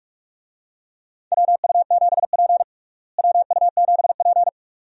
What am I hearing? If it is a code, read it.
Morse code, heard as a keyed tone, about 46 wpm, 700 Hz. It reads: WU7P WU7P
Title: USA Call Signs - 4 Characters - Repeated 2x - 45wpm